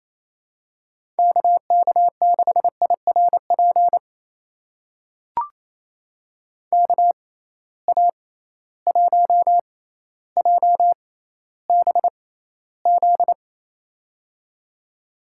Morse code, heard as a keyed tone, about 28 wpm, spaced as for 14 wpm, 700 Hz. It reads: KK6IRP KA1JBZ
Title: USA Call Signs - 6 Characters - 6x Character Spacing - 28wpm